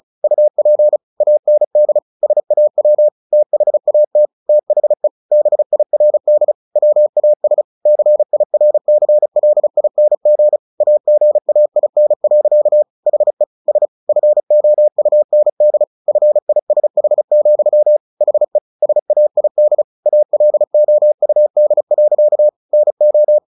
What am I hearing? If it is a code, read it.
Morse code, heard as a keyed tone, about 34 wpm, 600 Hz. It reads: UP AND SAW THAT THE BIRD WAS CIRCLING AGAIN. HE S FOUND FISH, HE SAID ALOUD. NO
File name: the old man and the sea / the old man and the sea 1 35wpm